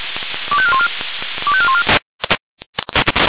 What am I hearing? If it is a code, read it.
Five-tone selective call (ZVEI), heard as quick runs of five tones, tones of about 70 ms. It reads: 25414 25414